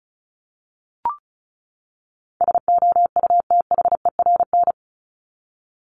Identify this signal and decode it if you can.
Morse code, heard as a keyed tone, about 34 wpm, 700 Hz. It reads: SOUTHERN
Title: Top 1751-2000 Words 35wpm